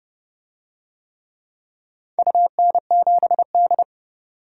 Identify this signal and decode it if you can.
Morse code, heard as a keyed tone, about 30 wpm, 700 Hz. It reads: UN7D
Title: DXCC-100 Call Signs - 4 Characters - 30wpm